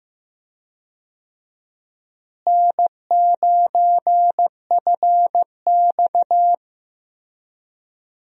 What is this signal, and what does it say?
Morse code, heard as a keyed tone, about 15 wpm, 700 Hz. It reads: N9FX
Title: USA Call Signs - 4 Characters - Repeated 2x - 15wpm